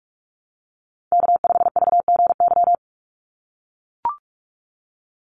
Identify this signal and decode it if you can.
Morse code, heard as a keyed tone, about 44 wpm, 700 Hz. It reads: K5VGY
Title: USA Call Signs 45wpm